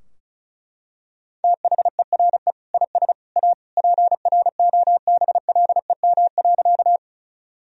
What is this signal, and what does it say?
Morse code, heard as a keyed tone, about 34 wpm, 700 Hz. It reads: THERE IS A PROBLEM.
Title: Sentences from Top 1000 Words - Review Part 2 - 35wpm